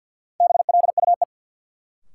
Morse code, heard as a keyed tone, about 53 wpm, 700 Hz. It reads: BLUE
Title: Sentences from Top 1200 Words - New Word Drill Alt - 50wpm